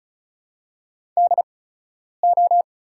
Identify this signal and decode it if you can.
Morse code, heard as a keyed tone, about 36 wpm, spaced as for 10 wpm, 700 Hz. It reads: DO
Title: 8x Character Spacing - Top 200 Words - 35wpm